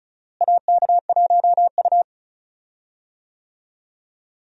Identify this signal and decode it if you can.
Morse code, heard as a keyed tone, about 36 wpm, 700 Hz. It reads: AK1U
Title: USA Call Signs - 4 Characters - Repeated 2x - 35wpm